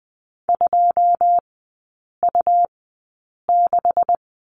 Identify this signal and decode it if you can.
Morse code, heard as a keyed tone, about 20 wpm, spaced as for 8 wpm, 700 Hz. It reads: 2U6
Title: 127 Lesson 21e - Number 6 - Three Characters 20wpm